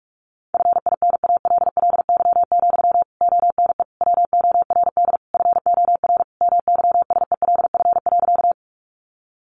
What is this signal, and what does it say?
Morse code, heard as a keyed tone, about 44 wpm, 700 Hz. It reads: FINALLY, ONE WORD FOR MYSELF.
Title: Sentences from Top 1300 Words 45wpm